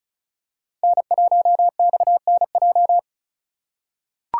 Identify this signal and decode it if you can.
Morse code, heard as a keyed tone, about 36 wpm, 700 Hz. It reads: N1XNJ
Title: USA Call Signs - 4x Character Spacing - 35wpm